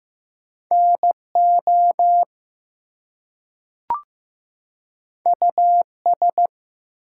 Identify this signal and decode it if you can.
Morse code, heard as a keyed tone, about 15 wpm, 700 Hz. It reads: NO US
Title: Two Letter Words - 15wpm